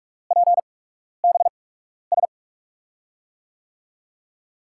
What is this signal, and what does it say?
Morse code, heard as a keyed tone, about 44 wpm, spaced as for 12 wpm, 690 Hz. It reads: PBS
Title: USA Call Signs - 5 Characters - 8x Character Spacing - 45wpm